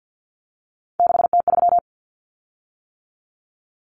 Morse code, heard as a keyed tone, about 50 wpm, 700 Hz. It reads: 6T3
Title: Common Cut Numbers - 3-Digit 50wpm